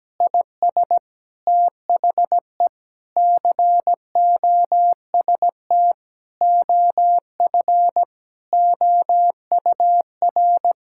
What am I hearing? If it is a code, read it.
Morse code, heard as a keyed tone, about 17 wpm, 700 Hz. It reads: IS THE COST OF OUR
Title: Sentences from Top 1000 Words - Review Part 5 - 17wpm